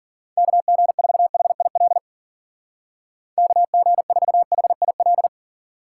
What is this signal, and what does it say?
Morse code, heard as a keyed tone, about 46 wpm, 700 Hz. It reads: KG4HIL KG4HIL
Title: Speed Racing - USA Call Signs 40wpm